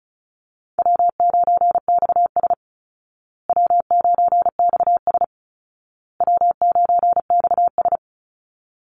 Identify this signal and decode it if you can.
Morse code, heard as a keyed tone, about 34 wpm, 700 Hz. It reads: W9XS W9XS W9XS
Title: Call Signs Repeated 2x - 35wpm